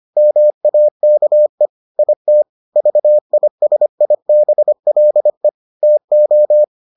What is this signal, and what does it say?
Morse code, heard as a keyed tone, about 25 wpm, 600 Hz. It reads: MAKE IT VISIBLE TO